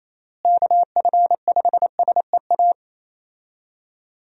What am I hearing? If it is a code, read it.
Morse code, heard as a keyed tone, about 27 wpm, 700 Hz. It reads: KF5SEA
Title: USA Call Signs - 6 Characters - Repeated 3x - 28wpm